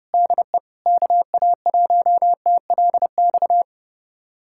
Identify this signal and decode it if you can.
Morse code, heard as a keyed tone, about 32 wpm, 700 Hz. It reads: DE KA1TLX
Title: USA Callsign DE Callsign - 6 Characters - 30wpm